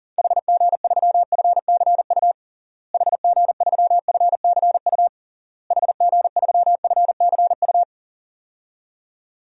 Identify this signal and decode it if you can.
Morse code, heard as a keyed tone, about 40 wpm, 700 Hz. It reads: HG3FCU HG3FCU HG3FCU
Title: DXCC-100 Call Signs - 6 Characters - Repeated 3x - 40wpm